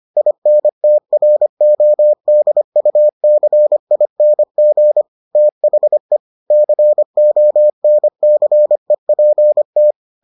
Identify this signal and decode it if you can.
Morse code, heard as a keyed tone, about 25 wpm, 600 Hz. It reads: INTRODUCING THE CONCEPT